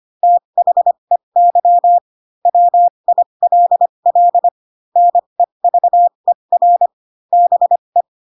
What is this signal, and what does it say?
Morse code, heard as a keyed tone, about 25 wpm, spaced as for 22 wpm, 700 Hz. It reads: THEY WILL NEVER BE
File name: Pinocchio / Pinocchio ch4 18 25 ik1hge